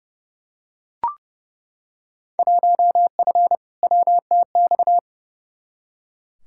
Morse code, heard as a keyed tone, about 29 wpm, 700 Hz. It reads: 1F WTX
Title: Field Day Exchanges Encore - 30wpm